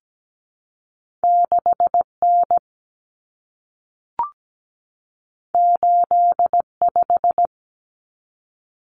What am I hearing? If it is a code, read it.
Morse code, heard as a keyed tone, about 17 wpm, 700 Hz. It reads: 6N 85
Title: Common Cut Numbers - 2-Digit 17wpm